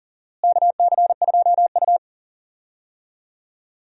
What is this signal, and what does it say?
Morse code, heard as a keyed tone, about 40 wpm, 700 Hz. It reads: KC2U
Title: USA Call Signs - 4 Characters - Repeated 2x - 40wpm